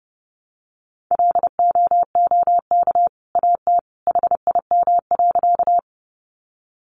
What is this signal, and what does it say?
Morse code, heard as a keyed tone, about 30 wpm, 700 Hz. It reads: LOOK AT HIM.
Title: Sentences from Top 100 Words 30wpm